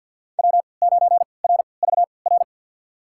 Morse code, heard as a keyed tone, about 51 wpm, spaced as for 28 wpm, 700 Hz. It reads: W9RVR
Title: USA Call Signs - 3x Character Spacing - 50wpm